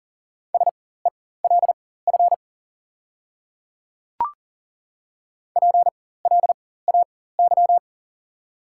Morse code, heard as a keyed tone, about 40 wpm, spaced as for 19 wpm, 700 Hz. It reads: SELF PLAY
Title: 4x Character Spacing - Top 200 Words - 40wpm